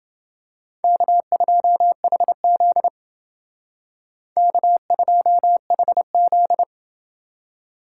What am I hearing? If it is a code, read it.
Morse code, heard as a keyed tone, about 28 wpm, 700 Hz. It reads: K2HZ K2HZ